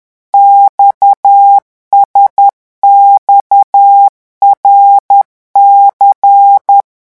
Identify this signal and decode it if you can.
Morse code, heard as a keyed tone, about 10 wpm, 800 Hz. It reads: XSXRC
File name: morse mp3 / revision11 lettres vitesse 10